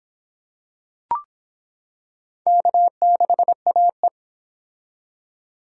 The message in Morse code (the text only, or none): K6AE